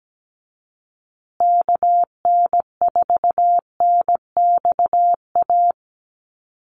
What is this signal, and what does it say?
Morse code, heard as a keyed tone, about 17 wpm, 700 Hz. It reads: KN4NXA
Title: USA Call Signs Repeated 3x 17wpm